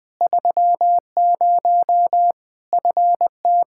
Morse code, heard as a keyed tone, about 21 wpm, 700 Hz. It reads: 30 FT
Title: Practice QSO - Exchange 2 - Info - 20wpm